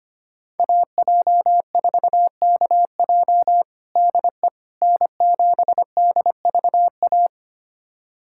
Morse code, heard as a keyed tone, about 25 wpm, 700 Hz. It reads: AJ4KJ DE N7DVA
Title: USA Callsign DE Callsign - 5 Characters - 25wpm